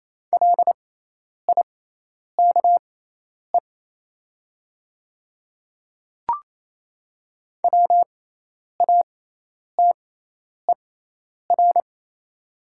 Morse code, heard as a keyed tone, about 27 wpm, spaced as for 10 wpm, 700 Hz. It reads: LIKE WATER
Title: 6x Character Spacing - Top 500 Words - 28wpm